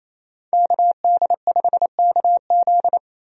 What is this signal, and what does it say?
Morse code, heard as a keyed tone, about 28 wpm, 700 Hz. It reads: KD5KZ
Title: USA Call Signs - 5 Characters - 28wpm